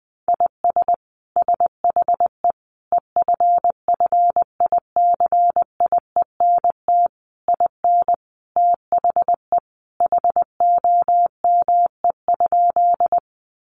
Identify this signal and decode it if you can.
Morse code, heard as a keyed tone, about 19 wpm, 700 Hz. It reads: IS SHE EFFICIENT IN THE HOME?
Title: Sentences from Top 1300 Words 20wpm